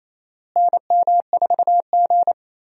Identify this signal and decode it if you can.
Morse code, heard as a keyed tone, about 28 wpm, 700 Hz. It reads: NM4G
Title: USA Call Signs - 4 Characters - Repeated 2x - 28wpm